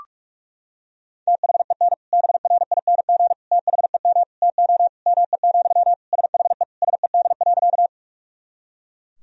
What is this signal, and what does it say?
Morse code, heard as a keyed tone, about 42 wpm, 690 Hz. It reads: THEN BRING THEM TO ME, SHE SED.
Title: Sentences from Top 500 Words and Abbreviations 45wpm